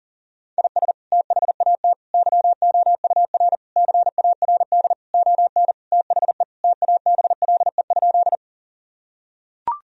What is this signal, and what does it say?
Morse code, heard as a keyed tone, about 40 wpm, 700 Hz. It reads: IS THAT YOUR CARD ON THE TABLE?
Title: Sentences from Top 1000 Words - Review Part 4 - 40wpm